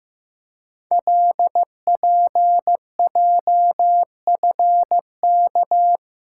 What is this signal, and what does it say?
Morse code, heard as a keyed tone, about 15 wpm, 700 Hz. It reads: LPJFK
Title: Code Groups - 5 Letters - 15wpm